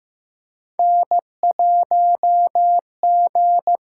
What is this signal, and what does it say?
Morse code, heard as a keyed tone, about 15 wpm, 700 Hz. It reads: N1G
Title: USA Call Signs - 5 Characters - 6x Character Spacing - 15wpm